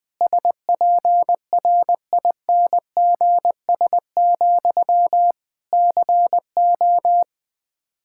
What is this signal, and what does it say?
Morse code, heard as a keyed tone, about 20 wpm, 700 Hz. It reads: SPRINGS, CO
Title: Practice QSO - Exchange 1 - Beginning - 20wpm